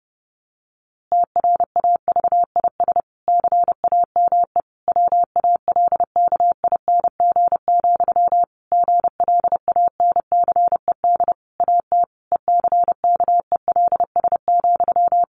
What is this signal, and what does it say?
Morse code, heard as a keyed tone, about 30 wpm, 700 Hz. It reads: TRAVIS CAME WALKING, GLANCED AT ECKELS,
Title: A Sound Of Thunder 30wpm